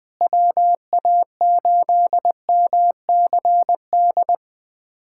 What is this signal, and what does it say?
Morse code, heard as a keyed tone, about 20 wpm, 700 Hz. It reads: WA8MCD
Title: USA Callsign DE Callsign 20wpm